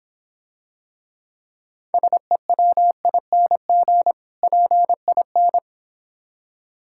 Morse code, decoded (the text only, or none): SEWING PIN